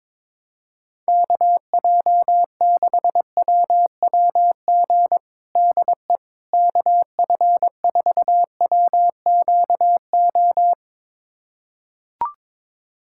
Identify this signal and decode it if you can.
Morse code, heard as a keyed tone, about 22 wpm, 700 Hz. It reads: KJ6WWG DE KF4WQO
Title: USA Callsign DE Callsign - 6 Characters - 22wpm